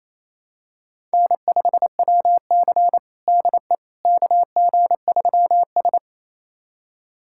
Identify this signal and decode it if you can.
Morse code, heard as a keyed tone, about 28 wpm, 700 Hz. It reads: N5WC DE KG3S
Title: USA Callsign DE Callsign - 4 Characters - 28wpm